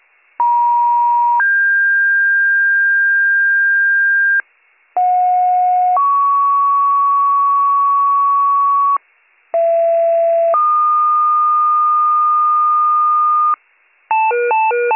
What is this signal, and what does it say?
Two-tone page, heard as a one-second tone, then a three-second tone, about 950 then 1600 Hz, about 730 then 1120 Hz, about 670 then 1220 Hz. It is heard from 0.4 to 13.6 s.